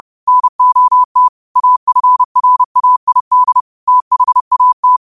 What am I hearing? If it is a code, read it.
Morse code, heard as a keyed tone, about 30 wpm, 1000 Hz. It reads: NOT AFRAID THAT